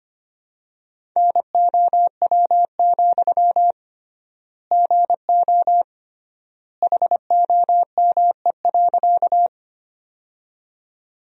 Morse code, heard as a keyed tone, about 25 wpm, 700 Hz. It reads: NOW, GO HOME.
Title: 3x Word-Spacing - Sentences from Top 200 Words - 25wpm